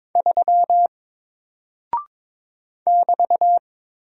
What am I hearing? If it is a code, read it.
Morse code, heard as a keyed tone, about 22 wpm, 700 Hz. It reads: 3 <BT>
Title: Single Character (Letter:Number:Punctuation:Prosign) Alt 22wpm